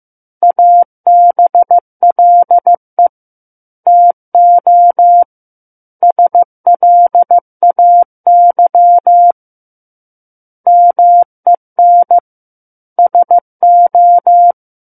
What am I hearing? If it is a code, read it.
Morse code, heard as a keyed tone, about 15 wpm, 700 Hz. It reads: ABLE TO SLAY MEN SO
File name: Book One / 06-Chapter 6 - 15 WPM